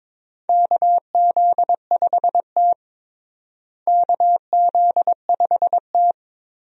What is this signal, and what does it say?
Morse code, heard as a keyed tone, about 22 wpm, 700 Hz. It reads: KZ5T KZ5T
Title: USA Call Signs - 4 Characters - Repeated 2x - 22wpm